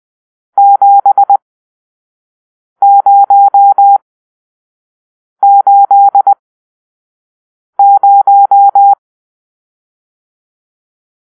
Morse code, heard as a keyed tone, about 20 wpm, spaced as for 5 wpm, 800 Hz. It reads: 7080